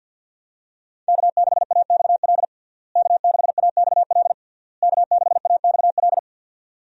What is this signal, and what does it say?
Morse code, heard as a keyed tone, about 52 wpm, 700 Hz. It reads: K6AXL K6AXL K6AXL
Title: USA Call Signs - 5 Characters - Repeated 3x - 50wpm